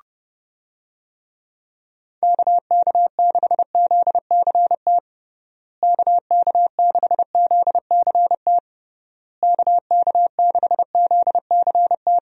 Morse code, heard as a keyed tone, about 30 wpm, 700 Hz. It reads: KK6ZCT KK6ZCT KK6ZCT
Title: USA Call Signs - 6 Characters - Repeated 3x - 30wpm